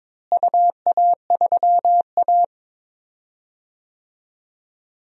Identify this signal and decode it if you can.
Morse code, heard as a keyed tone, about 21 wpm, 700 Hz. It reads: UA3A
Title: DXCC-100 Call Signs - 4 Characters - Repeated 2x - 22wpm